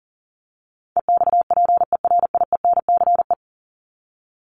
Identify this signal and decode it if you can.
Morse code, heard as a keyed tone, about 38 wpm, 700 Hz. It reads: EXPERIENCE
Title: Top 501-750 Words 40wpm